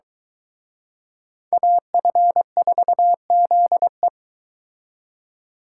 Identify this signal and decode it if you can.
Morse code, heard as a keyed tone, about 23 wpm, 700 Hz. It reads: AF4ZE